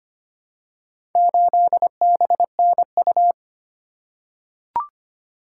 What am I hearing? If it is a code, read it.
Morse code, heard as a keyed tone, about 25 wpm, 700 Hz. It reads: 8BNU
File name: practice / 25wpm / Four Letter-Number Combinations - 25wpm